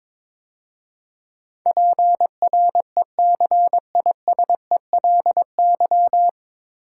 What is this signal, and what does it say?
Morse code, heard as a keyed tone, about 22 wpm, 700 Hz. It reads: PRECISELY
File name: 9 Letter Words - First 3 Letters Given - 22wpm